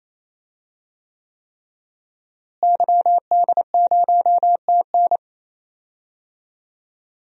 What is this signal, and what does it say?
Morse code, heard as a keyed tone, about 27 wpm, 700 Hz. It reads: YD0TN
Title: DXCC-100 Call Signs - 5 Characters - Repeated 3x - 28wpm